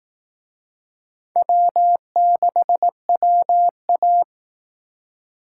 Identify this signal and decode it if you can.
Morse code, heard as a keyed tone, about 18 wpm, 700 Hz. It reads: W6WA
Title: Speed Racing - USA 1x2 Call Signs 15wpm